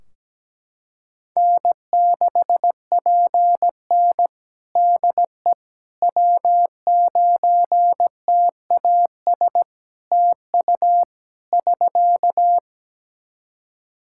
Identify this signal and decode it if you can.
Morse code, heard as a keyed tone, about 17 wpm, 700 Hz. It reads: N6PN DE W9TAS TU <SK>
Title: Practice QSO - Exchange 4 - Ending - 17wpm